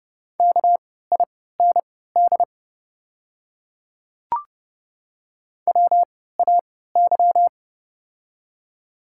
Morse code, heard as a keyed tone, about 29 wpm, spaced as for 17 wpm, 700 Hz. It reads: KIND WAY
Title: 3x Character Spacing - Top 200 Words - 30wpm